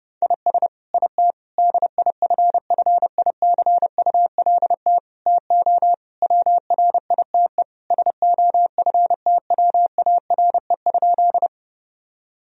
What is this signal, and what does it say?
Morse code, heard as a keyed tone, about 30 wpm, 700 Hz. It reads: IS IT DIFFICULT TO WRITE SOFTWARE?
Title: Sentences from Top 1000 Words - Review Part 3 - 30wpm